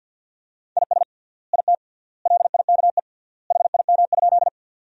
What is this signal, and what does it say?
Morse code, heard as a keyed tone, about 50 wpm, 700 Hz. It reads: IS IT LIKE HIM?
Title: 3x Word-Spacing - Sentences from Top 100 Words - 50wpm